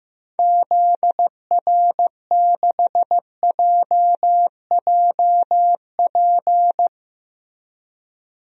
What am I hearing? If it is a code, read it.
Morse code, heard as a keyed tone, about 15 wpm, 700 Hz. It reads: ZR6JJP
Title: DXCC-100 Call Signs - 6 Characters - 15wpm